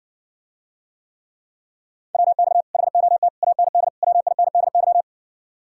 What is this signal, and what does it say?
Morse code, heard as a keyed tone, about 60 wpm, 700 Hz. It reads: WX HOT AND WINDY